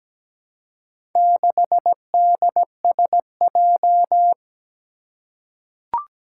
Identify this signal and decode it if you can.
Morse code, heard as a keyed tone, about 17 wpm, 700 Hz. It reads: 6DSJ
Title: Four Challenging Letter-Number Combinations 17wpm